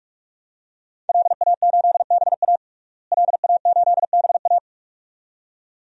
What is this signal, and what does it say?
Morse code, heard as a keyed tone, about 47 wpm, 700 Hz. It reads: LA8BA LA8BA
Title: DXCC-100 Call Signs - 5 Characters - Repeated 2x - 45wpm